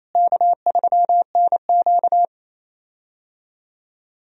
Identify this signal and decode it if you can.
Morse code, heard as a keyed tone, about 28 wpm, 700 Hz. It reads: K3NQ
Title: USA Call Signs - 4 Characters - Repeated 2x - 28wpm